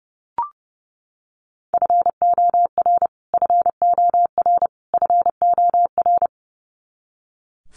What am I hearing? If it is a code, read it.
Morse code, heard as a keyed tone, about 30 wpm, 700 Hz. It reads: FOR FOR FOR
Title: Top 300 Words Repeated 30wpm